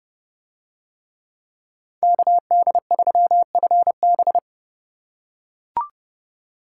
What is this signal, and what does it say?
Morse code, heard as a keyed tone, about 28 wpm, 700 Hz. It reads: KD3FB